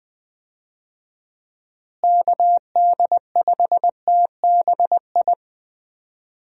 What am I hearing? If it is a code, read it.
Morse code, heard as a keyed tone, about 20 wpm, 700 Hz. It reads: KD5TBI
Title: USA Call Signs - 6 Characters - Repeated 2x - 20wpm